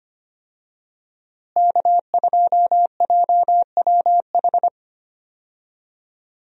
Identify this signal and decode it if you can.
Morse code, heard as a keyed tone, about 25 wpm, 700 Hz. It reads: K2JWH
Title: USA Call Signs - 5 Characters - Repeated 2x - 25wpm